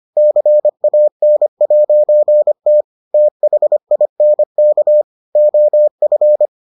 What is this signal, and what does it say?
Morse code, heard as a keyed tone, about 25 wpm, 600 Hz. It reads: CAN'T THINK OF